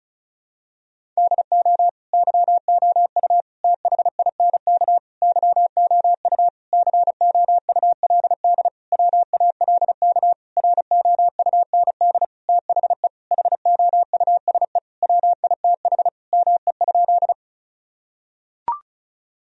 Morse code, heard as a keyed tone, about 34 wpm, 700 Hz. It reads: DO YOU THINK YOU COULD WALK ROUND THE HOUSE WITH ME?
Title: Sentences from Top 1000 Words - Review Part 1 - 35wpm